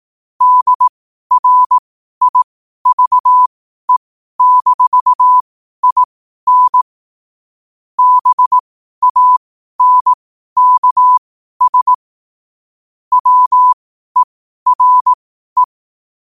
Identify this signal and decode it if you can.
Morse code, heard as a keyed tone, about 18 wpm, spaced as for 12 wpm, 1000 Hz. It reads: DRIVE-IN BANKS WERE
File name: qotdfiles / 12wpm